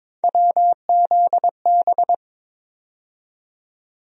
Morse code, heard as a keyed tone, about 21 wpm, 700 Hz. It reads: WZB
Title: USA Call Sign Postfixes - 3 Letters 22wpm